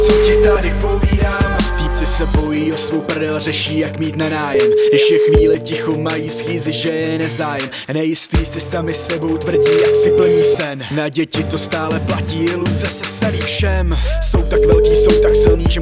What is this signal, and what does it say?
Ringback tone (Europe, 425 Hz), with speech.